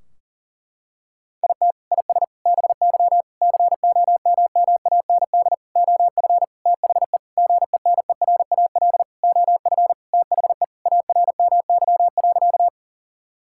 Morse code, heard as a keyed tone, about 40 wpm, 700 Hz. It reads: IT IS BY COMMAND OF THE GENERAL OF THE ARMY.
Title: Sentences from Top 1600 Words 40wpm